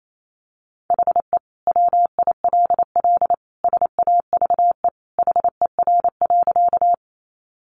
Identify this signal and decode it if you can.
Morse code, heard as a keyed tone, about 28 wpm, 700 Hz. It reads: HE WILL SAVE HER.
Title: Sentences from Top 1100 Words 28wpm